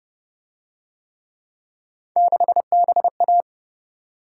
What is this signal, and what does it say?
Morse code, heard as a keyed tone, about 29 wpm, 700 Hz. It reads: 6BA